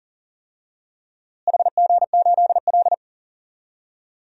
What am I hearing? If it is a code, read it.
Morse code, heard as a keyed tone, about 38 wpm, 700 Hz. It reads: HG8L
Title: DXCC-100 Call Signs - 4 Characters - Repeated 3x - 40wpm